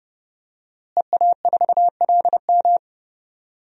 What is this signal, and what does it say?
Morse code, heard as a keyed tone, about 30 wpm, 700 Hz. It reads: EA4LM
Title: DXCC-100 Call Signs - 5 Characters - Repeated 3x - 30wpm